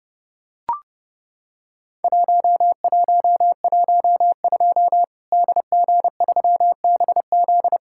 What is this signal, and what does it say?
Morse code, heard as a keyed tone, about 30 wpm, 700 Hz. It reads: 1112 DG3BZ
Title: Worked All Europe DX Contest - QTC Exchange - 30wpm